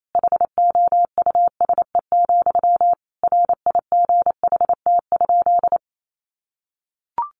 Morse code, heard as a keyed tone, about 28 wpm, 700 Hz. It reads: HOUSE, RIGHT?
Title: Sentences from Top 1200 Words 28wpm